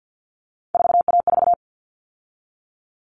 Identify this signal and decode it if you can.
Morse code, heard as a keyed tone, about 50 wpm, 700 Hz. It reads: 4A4
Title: All Cut Numbers - 3-Digit 50wpm